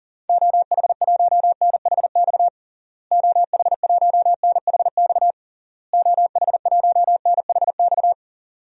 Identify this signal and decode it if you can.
Morse code, heard as a keyed tone, about 37 wpm, 700 Hz. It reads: OH1NHX OH1NHX OH1NHX
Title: DXCC-100 Call Signs - 6 Characters - Repeated 3x - 40wpm